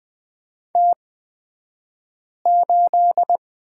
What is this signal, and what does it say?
Morse code, heard as a keyed tone, about 20 wpm, spaced as for 5 wpm, 700 Hz. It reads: T8
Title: 228 Lesson 37d - Letter x - Two Characters 20wpm